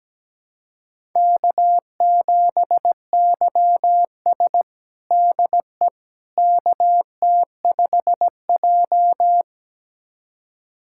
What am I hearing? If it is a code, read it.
Morse code, heard as a keyed tone, about 17 wpm, 700 Hz. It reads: K7YS DE KT5J
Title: USA Callsign DE Callsign - 4 Characters - 17wpm